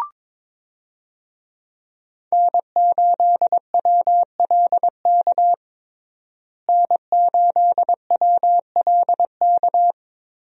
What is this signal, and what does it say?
Morse code, heard as a keyed tone, about 21 wpm, 700 Hz. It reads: N8WLK N8WLK
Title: USA Call Signs - 5 Characters - Repeated 3x - 22wpm